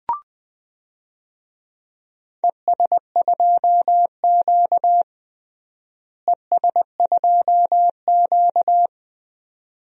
Morse code, heard as a keyed tone, about 20 wpm, 700 Hz. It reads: ES2Q ES2Q
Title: DXCC-100 Call Signs - 4 Characters - Repeated 2x - 20wpm